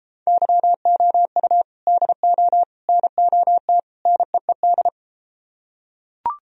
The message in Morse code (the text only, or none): YOU DO NOT NEED